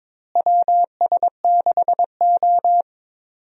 Morse code, heard as a keyed tone, about 21 wpm, 700 Hz. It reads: WS6O
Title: USA Call Signs - 4 Characters - Repeated 3x - 22wpm